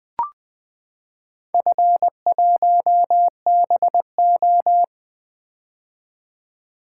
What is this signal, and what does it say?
Morse code, heard as a keyed tone, about 20 wpm, 700 Hz. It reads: F1BO
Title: Four Letter-Number Combinations - 20wpm